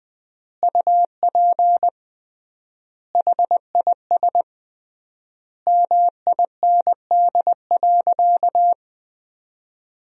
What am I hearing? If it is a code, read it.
Morse code, heard as a keyed tone, about 21 wpm, 700 Hz. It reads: UP HIS MIND.
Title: 3x Word-Spacing - Sentences from Top 500 Words - 20wpm